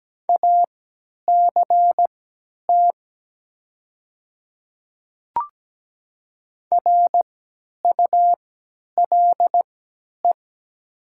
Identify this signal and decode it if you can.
Morse code, heard as a keyed tone, about 17 wpm, spaced as for 10 wpm, 700 Hz. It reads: ACT RULE
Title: 3x Character Spacing - Top 500 Words - 17wpm